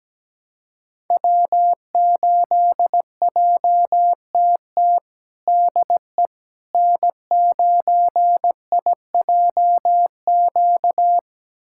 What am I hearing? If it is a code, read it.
Morse code, heard as a keyed tone, about 17 wpm, 700 Hz. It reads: W8JTT DE N9IJQ
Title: USA Callsign DE Callsign - 5 Characters - 17wpm